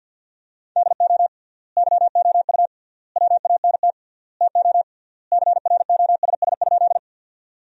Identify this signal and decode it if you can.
Morse code, heard as a keyed tone, about 52 wpm, 700 Hz. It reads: DO YOU WANT TO CROSS?
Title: 3x Word-Spacing - Sentences from Top 300 Words - 50wpm